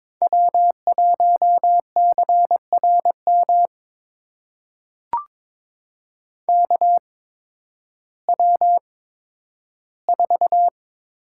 Morse code, heard as a keyed tone, about 22 wpm, spaced as for 12 wpm, 700 Hz. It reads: W1CRM KW4
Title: USA Call Signs - 5 Characters - 8x Character Spacing - 22wpm